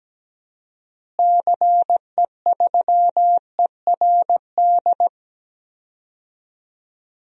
Morse code, heard as a keyed tone, about 17 wpm, 700 Hz. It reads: CE3ERD
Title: DXCC-100 Call Signs - 6 Characters - Repeated 2x - 17wpm